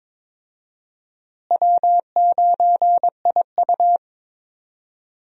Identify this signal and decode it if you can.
Morse code, heard as a keyed tone, about 21 wpm, 700 Hz. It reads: W9IU
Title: USA Call Signs - 4 Characters - Repeated 2x - 22wpm